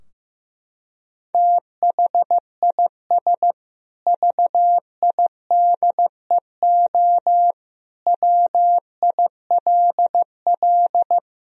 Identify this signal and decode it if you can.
Morse code, heard as a keyed tone, about 14 wpm, 700 Hz. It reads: THIS VIDEO WILL
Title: Sentences from Top 1000 Words - Review Part 4 - 15wpm